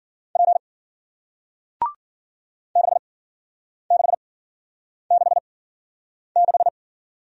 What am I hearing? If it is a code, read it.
Morse code, heard as a keyed tone, about 53 wpm, spaced as for 9 wpm, 700 Hz. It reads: P 6666